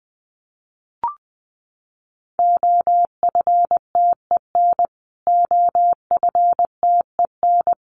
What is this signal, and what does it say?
Morse code, heard as a keyed tone, about 20 wpm, 700 Hz. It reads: OFTEN OFTEN
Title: Top 500 Words Repeated 20wpm